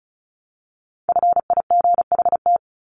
Morse code, heard as a keyed tone, about 34 wpm, 700 Hz. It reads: FIGHT
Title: ING Suffix - Top 100 Words with Base Words 35wpm